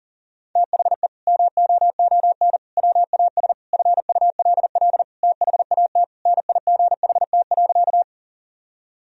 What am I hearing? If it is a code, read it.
Morse code, heard as a keyed tone, about 40 wpm, 700 Hz. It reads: THE MOON WAS FULL THAT NIGHT.
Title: Sentences from Top 1000 Words - Review Part 3 - 40wpm